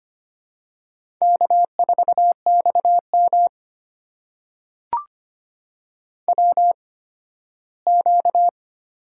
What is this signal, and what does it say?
Morse code, heard as a keyed tone, about 25 wpm, spaced as for 15 wpm, 700 Hz. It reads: K4XM WQ
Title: USA Call Signs - 4 Characters - 8x Character Spacing - 25wpm